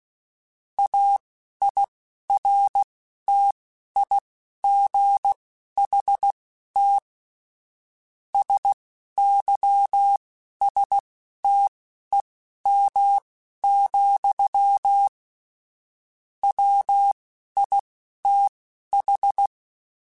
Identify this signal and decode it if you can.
Morse code, heard as a keyed tone, about 16 wpm, spaced as for 11 wpm, 800 Hz. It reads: AIRTIGHT SYSTEM, WITH
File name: Quotes / Quotes-mp3s / Airtight @10